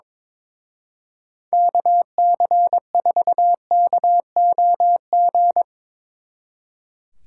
Morse code, heard as a keyed tone, about 24 wpm, 700 Hz. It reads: KC4KOG
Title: Speed Racing - USA 2x3 Call Signs 22wpm